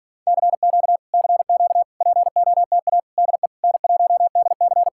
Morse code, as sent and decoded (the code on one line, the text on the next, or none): -.-. --.- / -.-. --.- / .--. --- - .- / -.. / -. .---- -.. -.-.
CQ CQ POTA DE N1DC